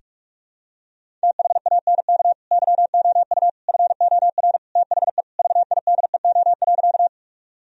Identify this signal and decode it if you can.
Morse code, heard as a keyed tone, about 46 wpm, 700 Hz. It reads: THANK YOU FOR THE VIDEO.
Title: Sentences from Top 1000 Words - Review Part 4 - 45wpm